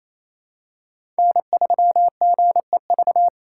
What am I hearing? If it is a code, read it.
Morse code, heard as a keyed tone, about 27 wpm, 700 Hz. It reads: N3GEV
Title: Code Groups - 5 Letter:Numbers - 28wpm